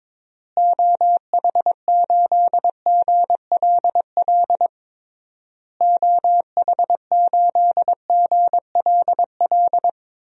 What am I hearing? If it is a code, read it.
Morse code, heard as a keyed tone, about 22 wpm, 700 Hz. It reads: OH8GLL OH8GLL
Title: DXCC-100 Call Signs - 6 Characters - Repeated 3x - 22wpm